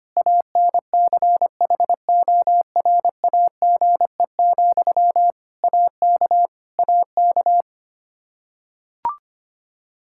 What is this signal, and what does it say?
Morse code, heard as a keyed tone, about 25 wpm, 700 Hz. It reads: ANCHORAGE, AK AK
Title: Practice QSO - Exchange 1 - Beginning - 25wpm